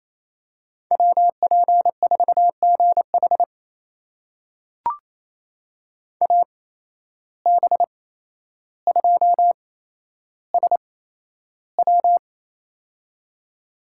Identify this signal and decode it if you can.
Morse code, heard as a keyed tone, about 28 wpm, spaced as for 12 wpm, 700 Hz. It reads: WP4GH AB2SW
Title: USA Call Signs - 5 Characters - 8x Character Spacing - 28wpm